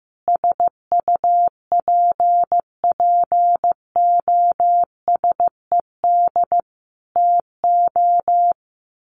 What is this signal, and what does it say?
Morse code, heard as a keyed tone, about 15 wpm, 700 Hz. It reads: SUPPOSED TO
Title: Sets of 5 Words Encore2 15wpm